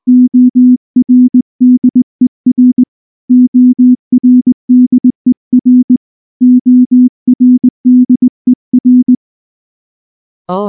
Morse code, heard as a keyed tone, about 18 wpm, 250 Hz. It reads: ORDER ORDER ORDER
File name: morse / morse-training-20WPM-9